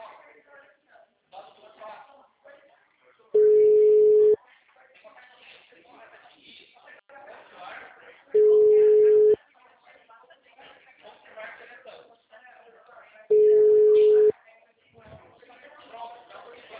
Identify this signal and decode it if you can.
Ringback tone (Europe, 425 Hz), with speech.
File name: out- -2014- - - .46316